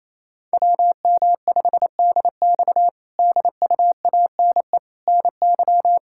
Morse code, heard as a keyed tone, about 28 wpm, 700 Hz. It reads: WM5DX DUANE NY
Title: K1USN SST QSO Practice - 28wpm